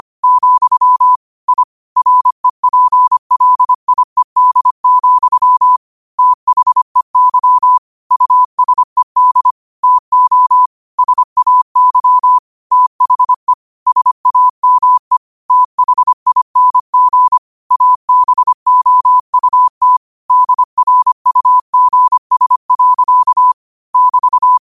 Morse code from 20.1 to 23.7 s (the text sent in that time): DRUGS.